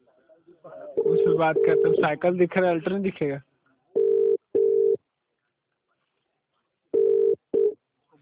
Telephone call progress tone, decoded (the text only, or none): ringback tone